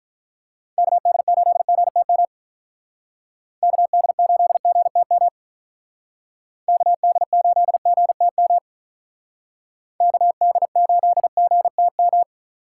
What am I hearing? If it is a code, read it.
Morse code, heard as a keyed tone, about 46 wpm, 700 Hz. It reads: KD8GTM KD8GTM KD8GTM KD8GTM